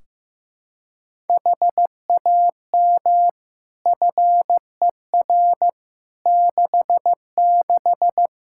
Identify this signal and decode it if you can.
Morse code, heard as a keyed tone, about 15 wpm, 700 Hz. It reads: HAM FER 66
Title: Practice QSO - Exchange 3 - More Info - 15wpm